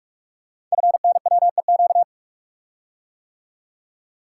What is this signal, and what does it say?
Morse code, heard as a keyed tone, about 44 wpm, 700 Hz. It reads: FNWEQ
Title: Code Groups - 5 Letters - 45wpm